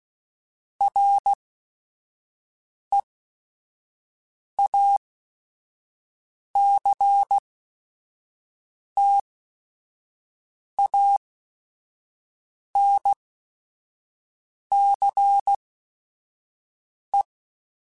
Morse code, heard as a keyed tone, about 16 wpm, spaced as for 5 wpm, 800 Hz. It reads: REACTANCE
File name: Technical Subjects / Technical Subjects-mp3s / XC @5